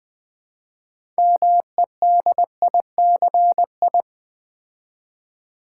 Morse code, heard as a keyed tone, about 20 wpm, 700 Hz. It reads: MEDICI
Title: Build A Word - 2 Letters At A Time - 8-Letter Words - 20wpm